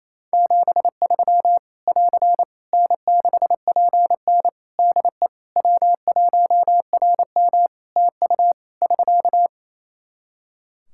Morse code, heard as a keyed tone, about 28 wpm, 700 Hz. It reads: 73 <AR> N6PN DE W1RM TU <SK>
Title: Practice QSO - Exchange 4 - Ending - 28wpm